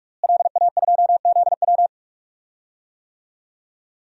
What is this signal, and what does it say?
Morse code, heard as a keyed tone, about 44 wpm, 700 Hz. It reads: LA2ZW
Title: DXCC-100 Call Signs - 5 Characters - Repeated 2x - 45wpm